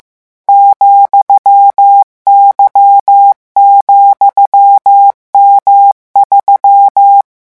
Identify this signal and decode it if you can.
Morse code, heard as a keyed tone, about 15 wpm, 800 Hz. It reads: ,Y,M3
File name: morse mp3 / revision13-vitesse 14 mots